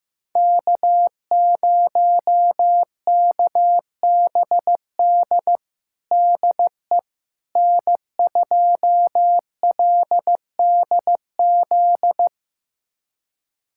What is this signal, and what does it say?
Morse code, heard as a keyed tone, about 15 wpm, 700 Hz. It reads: K0KBD DE N2LDZ
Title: USA Callsign DE Callsign - 5 Characters - 15wpm